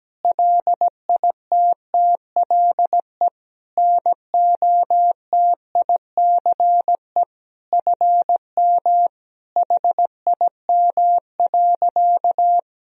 Morse code, heard as a keyed tone, about 17 wpm, 700 Hz. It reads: LITTLE NOTICE FM HIM.
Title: Sentences from Top 400 Words and Abbreviations 17wpm